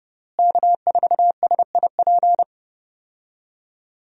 Morse code, heard as a keyed tone, about 32 wpm, 700 Hz. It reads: K4SIP
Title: USA Call Signs - 5 Characters - Repeated 2x - 30wpm